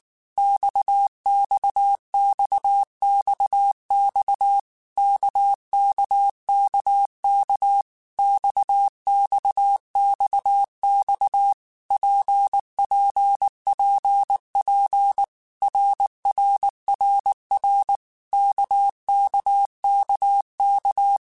Morse code, heard as a keyed tone, about 19 wpm, 800 Hz. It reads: XXXXX KKKK XXXX PPPP RRRR KKKK
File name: Basic Letter Groups By7Groups / RKPX / RKPX@18